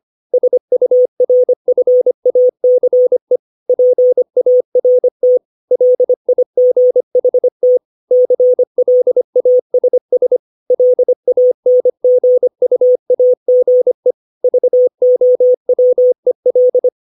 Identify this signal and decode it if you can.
Morse code, heard as a keyed tone, about 25 wpm, 500 Hz. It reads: SURFACE PART LIGHT CLASS LANGUAGE VOWEL
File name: 632 words / 632 Words 25WPM